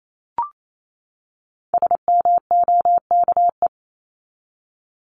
Morse code, heard as a keyed tone, about 28 wpm, 700 Hz. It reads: SMOKE
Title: Top 1501-1750 Words 28wpm